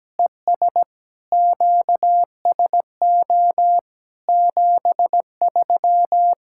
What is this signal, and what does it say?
Morse code, heard as a keyed tone, about 17 wpm, 700 Hz. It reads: ES QSO 73
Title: K1USN SST QSO Practice 17wpm